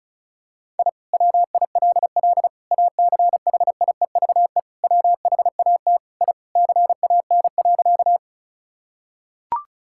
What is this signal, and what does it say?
Morse code, heard as a keyed tone, about 34 wpm, 700 Hz. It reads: I WILL ACHIEVE WHAT I CAN.
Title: Sentences from Top 1000 Words - Review Part 6 - 35wpm